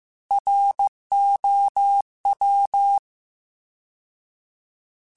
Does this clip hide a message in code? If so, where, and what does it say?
Yes: Morse code from 0.3 to 3.0 s, heard as a keyed tone, about 15 wpm, 800 Hz. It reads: ROW